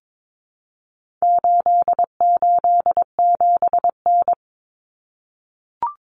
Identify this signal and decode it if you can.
Morse code, heard as a keyed tone, about 22 wpm, 700 Hz. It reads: 887N
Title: Common Cut Numbers - 4-Digit 22wpm